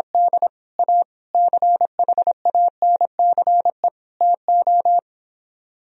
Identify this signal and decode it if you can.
Morse code, heard as a keyed tone, about 26 wpm, 700 Hz. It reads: D A CHANCE TO
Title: Speed-Racing - Sets of 4 Words 22wpm